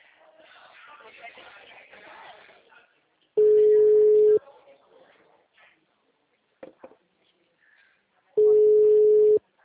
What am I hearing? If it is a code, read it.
Ringback tone (Europe, 425 Hz), with speech.
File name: out- -2019- - - .108